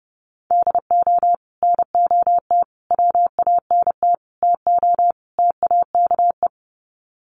Morse code, heard as a keyed tone, about 29 wpm, 700 Hz. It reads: DO NOT WANT TO TAKE
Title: Sets of 5 Words Encore2 30wpm